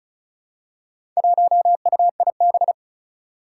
Morse code, heard as a keyed tone, about 36 wpm, 700 Hz. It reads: 1UIB
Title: Four Challenging Letter-Number Combinations 35wpm